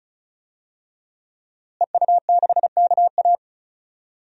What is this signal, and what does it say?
Morse code, heard as a keyed tone, about 34 wpm, 700 Hz. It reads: EU6KA
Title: DXCC-100 Call Signs - 5 Characters - Repeated 2x - 35wpm